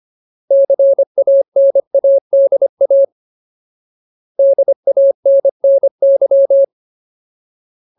Morse code, heard as a keyed tone, about 25 wpm, 550 Hz. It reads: CANADA DANNY
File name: Ultimate Word List / 25WPM Ultimate